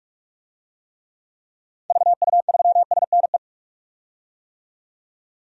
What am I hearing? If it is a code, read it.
Morse code, heard as a keyed tone, about 46 wpm, 700 Hz. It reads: VU3SNE
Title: DXCC-100 Call Signs - 6 Characters - 45wpm